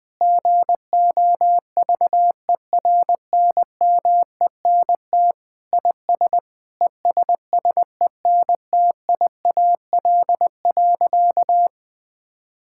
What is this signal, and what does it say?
Morse code, heard as a keyed tone, about 20 wpm, 700 Hz. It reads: GOVERNMENT IS ESSENTIAL.
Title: Sentences with 7-Word Phrases - 20wpm